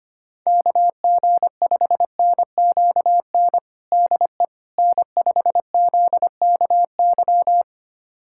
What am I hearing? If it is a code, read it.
Morse code, heard as a keyed tone, about 25 wpm, 700 Hz. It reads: KG5NQN DE N5ZKY
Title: USA Callsign DE Callsign 25wpm